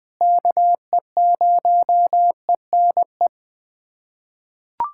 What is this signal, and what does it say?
Morse code, heard as a keyed tone, about 20 wpm, 700 Hz. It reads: KE0ENE
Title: USA Callsign DE Callsign 20wpm